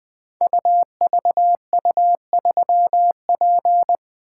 Morse code, heard as a keyed tone, about 20 wpm, 700 Hz. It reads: UVU3P
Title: Code Groups - 5 Letter:Numbers - 20wpm